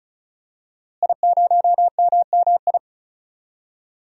Morse code, heard as a keyed tone, about 34 wpm, 700 Hz. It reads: I0MMI
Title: DXCC-100 Call Signs - 5 Characters - Repeated 3x - 35wpm